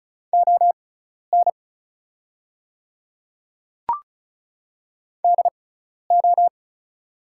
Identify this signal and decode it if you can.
Morse code, heard as a keyed tone, about 36 wpm, spaced as for 12 wpm, 690 Hz. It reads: ON DO
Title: 6x Character Spacing - Top 200 Words - 35wpm